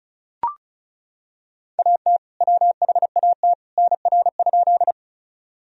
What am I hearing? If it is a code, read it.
Morse code, heard as a keyed tone, about 36 wpm, 700 Hz. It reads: AT WHAT NR?
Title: Sentences from Top 100 Words and Abbreviations 35wpm